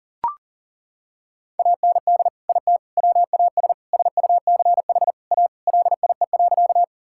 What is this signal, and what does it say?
Morse code, heard as a keyed tone, about 37 wpm, 700 Hz. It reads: AND IT WAS SUCH A LIE.
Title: Sentences from Top 1600 Words 40wpm